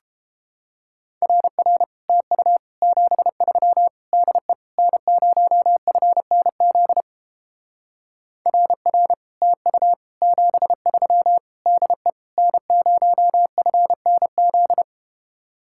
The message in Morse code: RR TU 73 DE N0FNZ RR TU 73 DE N0FNZ

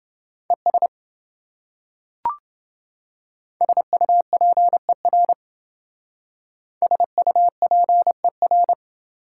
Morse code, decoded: ES SUPER SUPER

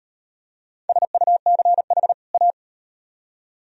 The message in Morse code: SUCH A

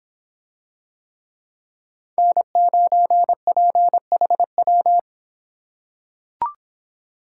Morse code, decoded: N9PHW

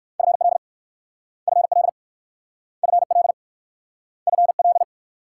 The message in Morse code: FL FL FL FL

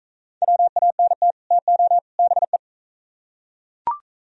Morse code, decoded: WANT TO BE

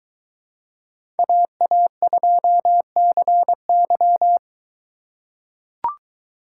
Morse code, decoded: AA2CY

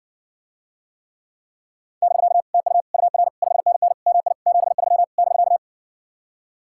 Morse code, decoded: <BK> TU UR 5NN MI 73 <BK>

